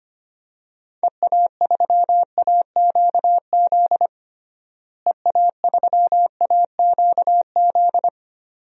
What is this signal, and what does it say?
Morse code, heard as a keyed tone, about 25 wpm, 700 Hz. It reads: EA3AQZ EA3AQZ
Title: DXCC-100 Call Signs - 6 Characters - Repeated 2x - 25wpm